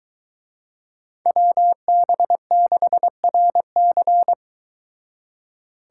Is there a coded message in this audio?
Yes: Morse code from 1.3 to 4.3 s, heard as a keyed tone, about 22 wpm, 700 Hz. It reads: WB6RC